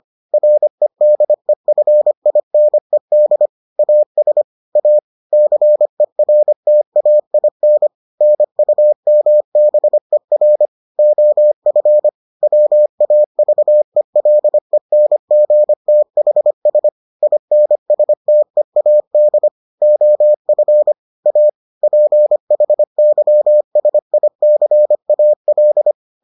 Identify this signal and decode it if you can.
Morse code, heard as a keyed tone, about 25 wpm, 600 Hz. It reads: REDEFINED AS A CERTAIN NUMBER OF WAVELENGTHS INSTEAD OF A PHYSICAL